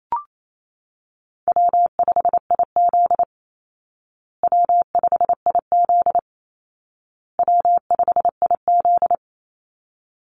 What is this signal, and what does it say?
Morse code, heard as a keyed tone, about 28 wpm, 700 Hz. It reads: W5IZ W5IZ W5IZ
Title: Easy Call Signs Repeated 3x - 28wpm